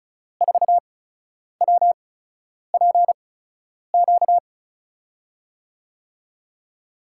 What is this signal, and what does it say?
Morse code, heard as a keyed tone, about 34 wpm, spaced as for 10 wpm, 700 Hz. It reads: 4WPQ
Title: USA Call Signs - 5 Characters - 8x Character Spacing - 35wpm